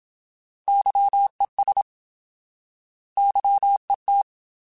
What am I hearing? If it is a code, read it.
Morse code, heard as a keyed tone, about 27 wpm, 800 Hz. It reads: YES YET
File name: Alphabetical / Alphabetical mp3s / Ys @25